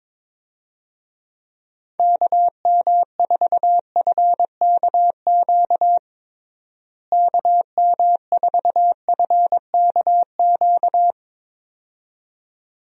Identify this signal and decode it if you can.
Morse code, heard as a keyed tone, about 22 wpm, 700 Hz. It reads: KM4FKQ KM4FKQ
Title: USA Call Signs - 6 Characters - Repeated 2x - 22wpm